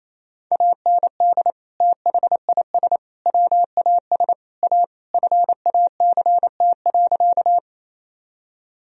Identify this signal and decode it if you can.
Morse code, heard as a keyed tone, about 27 wpm, 700 Hz. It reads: AND THIS WAS A FACT.
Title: Sentences from Top 1000 Words - Review Part 3 - 28wpm